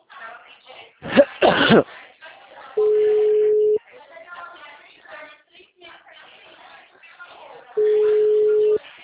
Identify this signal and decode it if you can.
Ringback tone (Europe, 425 Hz), with speech.